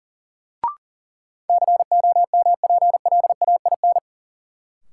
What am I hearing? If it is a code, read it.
Morse code, heard as a keyed tone, about 40 wpm, 700 Hz. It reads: COMPLAIN
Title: Sentences from Top 1300 Words - New Word Drill - 40wpm